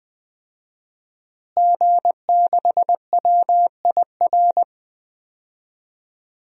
Morse code, heard as a keyed tone, about 20 wpm, 700 Hz. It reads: G6WIR
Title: DXCC-100 Call Signs - 5 Characters - Repeated 2x - 20wpm